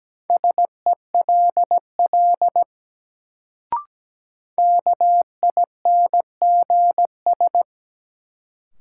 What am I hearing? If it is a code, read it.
Morse code, heard as a keyed tone, about 17 wpm, 700 Hz. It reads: SELL KINGS
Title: Sentences from Top 1000 Words - Review Part 6 - New Word Drill - 17wpm